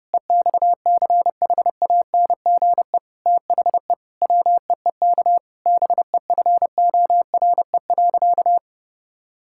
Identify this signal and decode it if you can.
Morse code, heard as a keyed tone, about 30 wpm, 700 Hz. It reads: EXCHANGE THE WEEK BEFORE.
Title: Sentences from Top 1000 Words - Review Part 6 - 30wpm